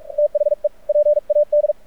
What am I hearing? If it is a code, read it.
Morse code, heard as a keyed tone, about 43 wpm, 590 Hz. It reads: THE PAD
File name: archived / CRSnet